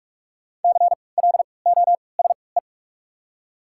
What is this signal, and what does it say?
Morse code, heard as a keyed tone, about 44 wpm, spaced as for 25 wpm, 690 Hz. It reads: CLOSE
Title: 3x Character Spacing - Top 300 Words - 45wpm